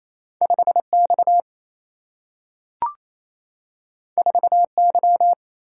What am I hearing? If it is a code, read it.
Morse code, heard as a keyed tone, about 28 wpm, 700 Hz. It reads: 5X 4Y
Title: Single-Letter Single-Number 28wpm